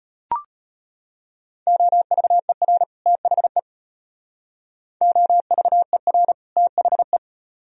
Morse code, heard as a keyed tone, about 33 wpm, 700 Hz. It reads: OVER THE OVER THE